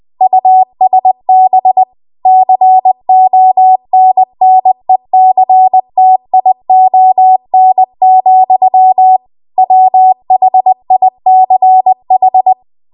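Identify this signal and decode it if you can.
Morse code, heard as a keyed tone, about 20 wpm, 750 Hz. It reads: USB CONNECTION, WHICH
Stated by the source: Morse code practice at 20 WPM.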